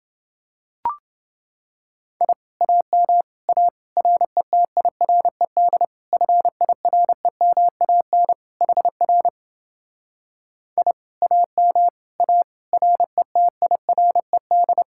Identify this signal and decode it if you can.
Morse code, heard as a keyed tone, about 29 wpm, 700 Hz. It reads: I AM A RETIRED FIREMAN HR I AM A RETIRED